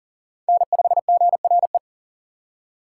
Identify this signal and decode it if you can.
Morse code, heard as a keyed tone, about 50 wpm, 700 Hz. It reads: N5GRE